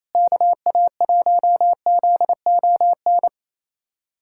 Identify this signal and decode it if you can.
Morse code, heard as a keyed tone, about 28 wpm, 700 Hz. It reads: KA1ZON
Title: USA Callsign DE Callsign - 6 Characters - 28wpm